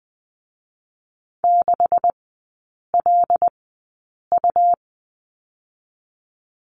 Morse code, heard as a keyed tone, about 20 wpm, spaced as for 8 wpm, 700 Hz. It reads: 6LU
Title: 127 Lesson 21e - Number 6 - Three Characters 20wpm